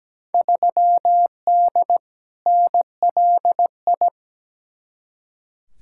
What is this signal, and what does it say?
Morse code, heard as a keyed tone, about 17 wpm, 700 Hz. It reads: 3D NLI
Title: Field Day Exchanges Encore - 17wpm